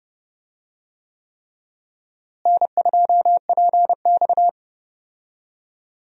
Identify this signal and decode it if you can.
Morse code, heard as a keyed tone, about 32 wpm, 700 Hz. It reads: N2PX
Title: USA Call Signs - 4 Characters - Repeated 3x - 30wpm